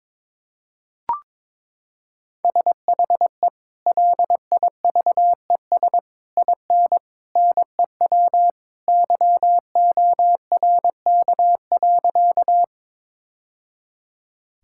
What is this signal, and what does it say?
Morse code, heard as a keyed tone, about 21 wpm, 700 Hz. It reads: SHE LIVES IN NEW YORK.
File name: practice / 22wpm / Sentences with 3-Word Phrases - 22wpm